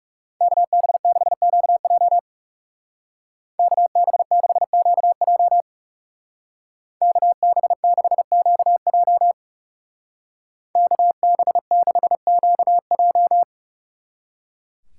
Morse code, heard as a keyed tone, about 36 wpm, 700 Hz. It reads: KB6QJ KB6QJ KB6QJ KB6QJ
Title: Speed Racing - USA 2x2 Call Signs 30wpm